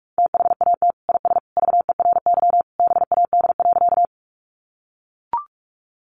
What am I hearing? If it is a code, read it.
Morse code, heard as a keyed tone, about 44 wpm, 700 Hz. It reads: THAT IS VERY BAD.
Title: Sentences from Top 700 Words 45wpm